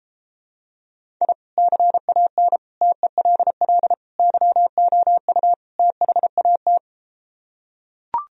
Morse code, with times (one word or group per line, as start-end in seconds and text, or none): 1.2-1.3 I
1.6-2.6 CAN
2.8-3.9 TELL
4.2-5.5 YOU
5.8-6.8 THAT